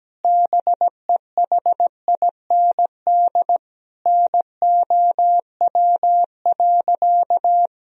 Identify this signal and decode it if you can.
Morse code, heard as a keyed tone, about 17 wpm, 700 Hz. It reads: BEHIND NOW.
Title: Sentences from Top 1000 Words - Review Part 1 - 17wpm